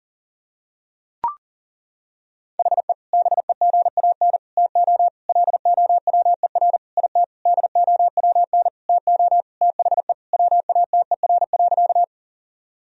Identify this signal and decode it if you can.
Morse code, heard as a keyed tone, about 37 wpm, 700 Hz. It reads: HE BEGAN TO LOWER IT DOWN TO THE WATER.
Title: Sentences from Top 1000 Words - Review Part 4 - 40wpm